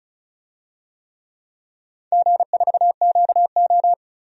Morse code, heard as a keyed tone, about 34 wpm, 700 Hz. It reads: G4QO
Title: DXCC-100 Call Signs - 4 Characters - 35wpm